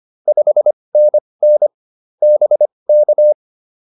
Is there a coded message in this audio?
Yes: Morse code from 0.3 to 3.3 s, heard as a keyed tone, about 25 wpm, spaced as for 20 wpm, 590 Hz. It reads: 5NN BK